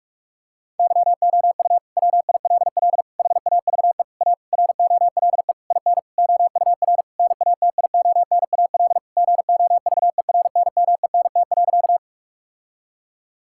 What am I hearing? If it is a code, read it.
Morse code, heard as a keyed tone, about 41 wpm, 700 Hz. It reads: YOU WILL HAVE A ROLE IN OUR NATIONAL GOVERNMENT.
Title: Sentences from Top 1000 Words - Review Part 6 - 45wpm